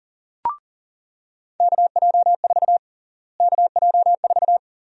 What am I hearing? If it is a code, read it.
Morse code, heard as a keyed tone, about 37 wpm, 700 Hz. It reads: KJ4 KJ4
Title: USA Call Sign Prefixes - Repeated 2x - 40wpm